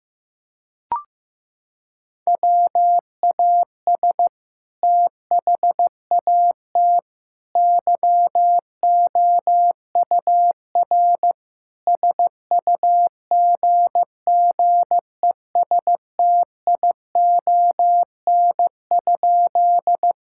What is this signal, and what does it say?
Morse code, heard as a keyed tone, about 15 wpm, 700 Hz. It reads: WAS THAT YOUR SUGGESTION?
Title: Sentences from Top 1600 Words 15wpm